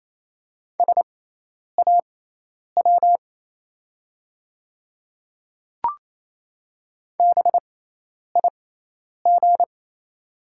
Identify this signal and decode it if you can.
Morse code, heard as a keyed tone, about 27 wpm, spaced as for 10 wpm, 700 Hz. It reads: SAW BIG
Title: 6x Character Spacing - Top 300 Words - 28wpm